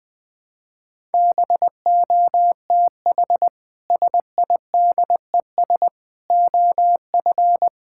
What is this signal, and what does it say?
Morse code, heard as a keyed tone, about 20 wpm, 700 Hz. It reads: BOTH SIDES OF
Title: Build A Phrase - 4-Word Phrases - 20wpm